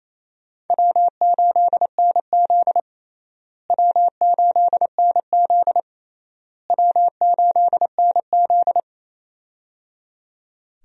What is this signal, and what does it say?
Morse code, heard as a keyed tone, about 28 wpm, 700 Hz. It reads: W8NZ W8NZ W8NZ
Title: USA Call Signs - 4 Characters - Repeated 3x - 28wpm